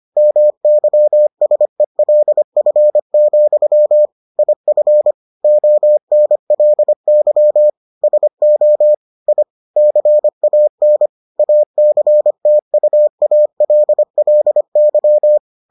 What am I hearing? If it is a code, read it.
Morse code, heard as a keyed tone, about 25 wpm, 600 Hz. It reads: MYSELF, IF ONLY SO I CAN ACTUALLY